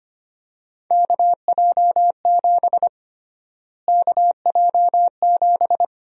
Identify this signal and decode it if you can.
Morse code, heard as a keyed tone, about 25 wpm, 700 Hz. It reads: KJ7 KJ7
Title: USA Call Sign Prefixes - Repeated 2x - 25wpm